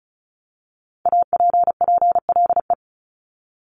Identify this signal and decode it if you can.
Morse code, heard as a keyed tone, about 36 wpm, 700 Hz. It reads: APPLE
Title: Multi-letter Phonemes 35wpm